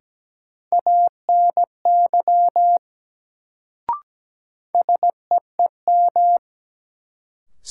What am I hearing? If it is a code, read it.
Morse code, heard as a keyed tone, about 17 wpm, 700 Hz. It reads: ANY SEEM
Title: Sets of 3 Words - New Word Drill - 17wpm